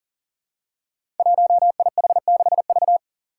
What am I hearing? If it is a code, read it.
Morse code, heard as a keyed tone, about 40 wpm, 700 Hz. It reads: 1IH6V
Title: Five Challenging Letter-Number Combinations 40wpm